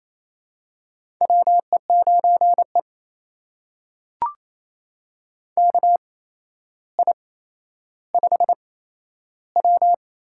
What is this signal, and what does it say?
Morse code, heard as a keyed tone, about 28 wpm, spaced as for 12 wpm, 700 Hz. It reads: WE9E KI5W
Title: USA Call Signs - 4 Characters - 8x Character Spacing - 28wpm